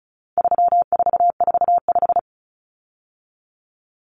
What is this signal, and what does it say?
Morse code, heard as a keyed tone, about 34 wpm, 700 Hz. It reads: 3445
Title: Numbers - 4-Digit 35wpm